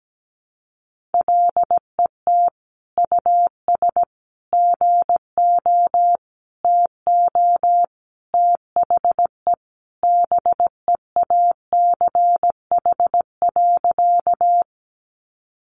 Morse code, heard as a keyed tone, about 17 wpm, 700 Hz. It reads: LET US GO TO THE BEACH.
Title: Sentences from Top 1500 Words 17wpm